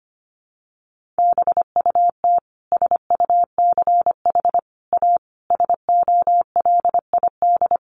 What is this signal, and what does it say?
Morse code, heard as a keyed tone, about 25 wpm, 700 Hz. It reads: BUT SUCH A SOLID
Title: Sentences from Top 1500 Words 25wpm